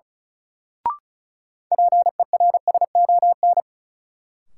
Morse code, heard as a keyed tone, about 36 wpm, 700 Hz. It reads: PERSON
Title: Sentences from Top 400 Words - New Word Drill - 35wpm